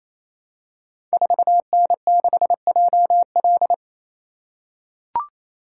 Morse code, heard as a keyed tone, about 28 wpm, 700 Hz. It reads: 4N6JL
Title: Code Groups - 5 Letter:Numbers - 28wpm